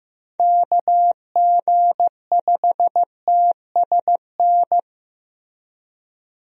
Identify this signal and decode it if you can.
Morse code, heard as a keyed tone, about 15 wpm, 700 Hz. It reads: KG5TSN
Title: USA Call Signs - 6 Characters - Repeated 2x - 15wpm